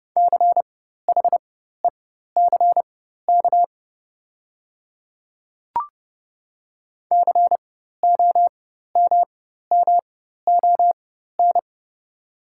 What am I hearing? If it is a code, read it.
Morse code, heard as a keyed tone, about 32 wpm, spaced as for 14 wpm, 690 Hz. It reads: CHECK COMMON
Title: 4x Character Spacing - Top 500 Words - 30wpm